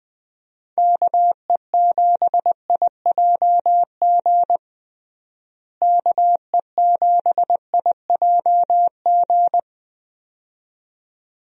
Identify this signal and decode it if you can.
Morse code, heard as a keyed tone, about 20 wpm, 700 Hz. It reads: KE7IJG KE7IJG
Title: USA Call Signs - 6 Characters - Repeated 2x - 20wpm